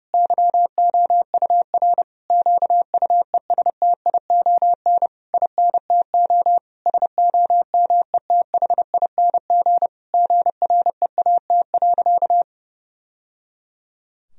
Morse code, heard as a keyed tone, about 32 wpm, 700 Hz. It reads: YOUR QUESTION INTO SOMETHING GREAT.
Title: Sentences from Top 1000 Words - Review Part 2 - 30wpm